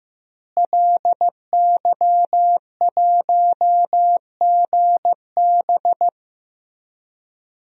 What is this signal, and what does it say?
Morse code, heard as a keyed tone, about 15 wpm, 700 Hz. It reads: LY1GB
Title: DXCC-100 Call Signs - 5 Characters - Repeated 3x - 15wpm